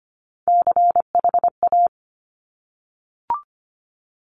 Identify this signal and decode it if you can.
Morse code, heard as a keyed tone, about 25 wpm, 700 Hz. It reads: CHA
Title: Top Three Letter Combinations in Words 25wpm